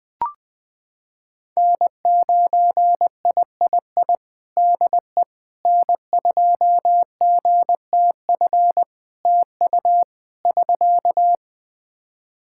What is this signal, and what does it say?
Morse code, heard as a keyed tone, about 20 wpm, 700 Hz. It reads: N9III DE N2GTF TU <SK>
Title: Practice QSO - Exchange 4 - Ending - 20wpm